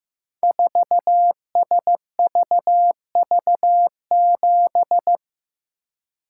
Morse code, heard as a keyed tone, about 15 wpm, 700 Hz. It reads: 4SVV7
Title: Code Groups - 5 Letter:Numbers - 15wpm